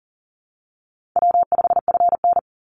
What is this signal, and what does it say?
Morse code, heard as a keyed tone, about 40 wpm, 700 Hz. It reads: W5FN
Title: Call Signs Repeated 2x - 40wpm